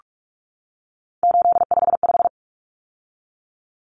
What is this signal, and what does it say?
Morse code, heard as a keyed tone, about 44 wpm, 700 Hz. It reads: 855
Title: Common Cut Numbers - 3-Digit 45wpm